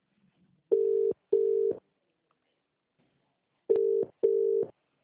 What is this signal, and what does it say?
Ringback tone (UK), 400 + 450 Hz.